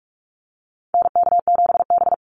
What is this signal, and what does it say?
Morse code, heard as a keyed tone, about 44 wpm, 700 Hz. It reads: NK7B
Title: Call Signs - 45wpm